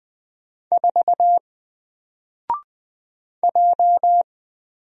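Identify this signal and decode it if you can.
Morse code, heard as a keyed tone, about 20 wpm, 700 Hz. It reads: 4 J
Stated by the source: Single letter and number Morse drill at 20 wpm, from N4LCD.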